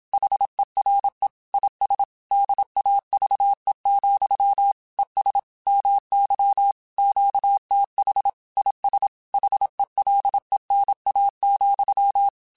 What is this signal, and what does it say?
Morse code, heard as a keyed tone, about 27 wpm, 800 Hz. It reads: HERE IS DAVE, ES MY QTH IS HELENA,
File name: QSOs / QSOs-mp3s / WD0FBT 1@25